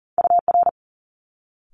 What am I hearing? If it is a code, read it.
Morse code, heard as a keyed tone, about 42 wpm, 700 Hz. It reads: UR
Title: Multi-letter Phonemes Quick Fire 40wpm